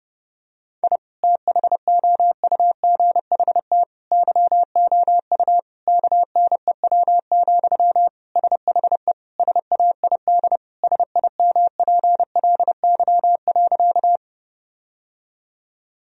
Morse code, heard as a keyed tone, about 30 wpm, 700 Hz. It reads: I THOUGHT YOU KNEW, SHE SAID SIMPLY.
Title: Sentences from Top 1000 Words - Review Part 4 - 30wpm